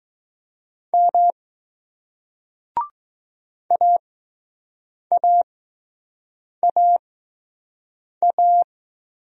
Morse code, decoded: M AAAA

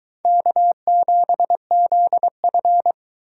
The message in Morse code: K7ZF

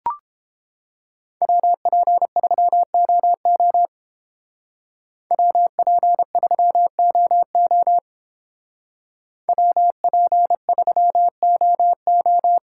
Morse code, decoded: WP3OO WP3OO WP3OO

